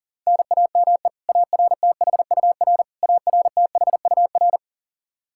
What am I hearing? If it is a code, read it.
Morse code, heard as a keyed tone, about 40 wpm, 700 Hz. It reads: NAME ARTHUR ARTHUR
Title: Practice QSO - Exchange 1 - Beginning - 40wpm